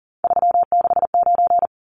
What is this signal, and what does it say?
Morse code, heard as a keyed tone, about 40 wpm, 700 Hz. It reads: 369
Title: Numbers - 3-Digit 40wpm